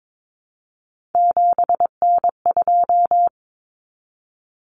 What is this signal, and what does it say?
Morse code, heard as a keyed tone, about 22 wpm, 700 Hz. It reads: 7N2
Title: Common Cut Numbers - 3-Digit 22wpm